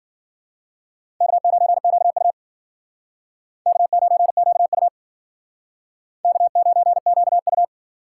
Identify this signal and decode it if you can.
Morse code, heard as a keyed tone, about 54 wpm, 700 Hz. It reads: K9QU K9QU K9QU